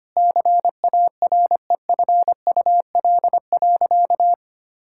Morse code, heard as a keyed tone, about 25 wpm, 700 Hz. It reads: CAREFUL.
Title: Sentences from Top 1600 Words 25wpm